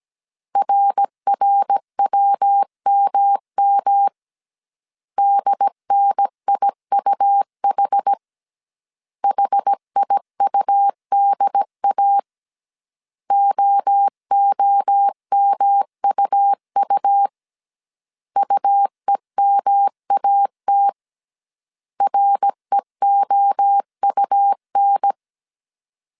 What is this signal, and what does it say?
Morse code, heard as a keyed tone, about 17 wpm, 800 Hz. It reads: RRWMM DNIUH HIUDA OOMUU UEMAT REOUN